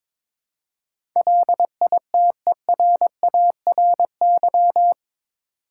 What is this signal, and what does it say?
Morse code, heard as a keyed tone, about 22 wpm, 700 Hz. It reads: LITERARY
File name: practice / 22wpm / Eight Letter Words - 22wpm